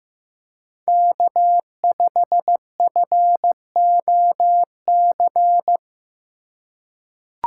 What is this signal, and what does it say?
Morse code, heard as a keyed tone, about 16 wpm, 700 Hz. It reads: K5FOC
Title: USA Call Signs - 5 Characters - 4x Character Spacing - 15wpm